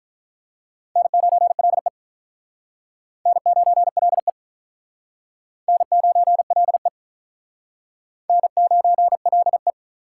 Morse code, heard as a keyed tone, about 46 wpm, 700 Hz. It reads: N9LE N9LE N9LE N9LE